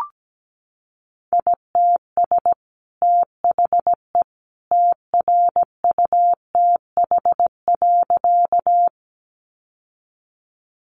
Morse code, heard as a keyed tone, about 17 wpm, 700 Hz. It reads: ITS THE TRUTH.
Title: Sentences from Top 800 Words 17wpm